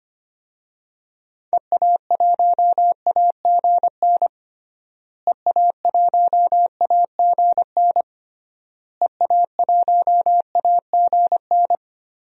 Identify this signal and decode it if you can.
Morse code, heard as a keyed tone, about 25 wpm, 700 Hz. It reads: EA1AGN EA1AGN EA1AGN
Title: DXCC-100 Call Signs - 6 Characters - Repeated 3x - 25wpm